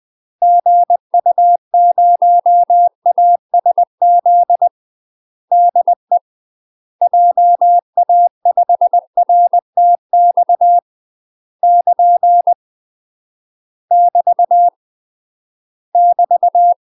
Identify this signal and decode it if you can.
Morse code, heard as a keyed tone, about 20 wpm, 700 Hz. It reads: GU0ASZ DE JA5RTX <KN> <BT> <BT>
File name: UK MASTER QSOs TRAINING / 20WPM / UK MASTER QSOs 0067